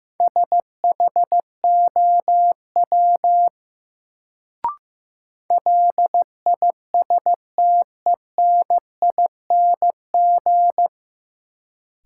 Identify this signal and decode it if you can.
Morse code, heard as a keyed tone, about 15 wpm, 700 Hz. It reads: SHOW LISTENING
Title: Sets of 5 Words - New Word Drill - 15wpm